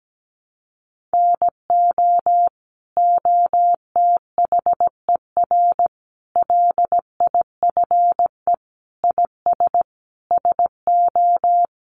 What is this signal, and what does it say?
Morse code, heard as a keyed tone, about 17 wpm, 700 Hz. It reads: NO OTHER LIFE IS SO
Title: Sentences from Top 1500 Words 17wpm